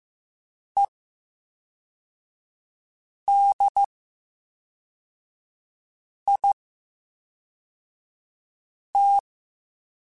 Morse code, heard as a keyed tone, about 15 wpm, 800 Hz. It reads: E D I T
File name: Special INGs / Special INGs-mp3s / VerySlowToFaster Ing @10